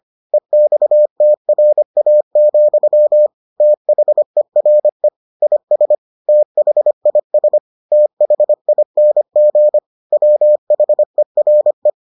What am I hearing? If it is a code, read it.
Morse code, heard as a keyed tone, about 25 wpm, 600 Hz. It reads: EXTRA, THERE IS THIS THING WHERE